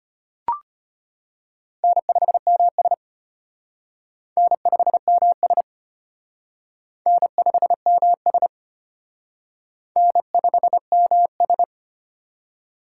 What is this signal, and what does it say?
Morse code, heard as a keyed tone, about 31 wpm, 700 Hz. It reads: N5MS N5MS N5MS N5MS
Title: Speed Racing - USA 1x2 Call Signs 25wpm